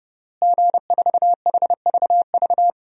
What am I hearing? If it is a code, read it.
Morse code, heard as a keyed tone, about 30 wpm, 700 Hz. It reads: G4HVV
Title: DXCC-100 Call Signs - 5 Characters - Repeated 3x - 30wpm